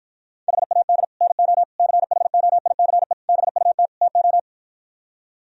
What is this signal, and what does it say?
Morse code, heard as a keyed tone, about 52 wpm, 700 Hz. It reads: HAD NO CHOICE BUT TO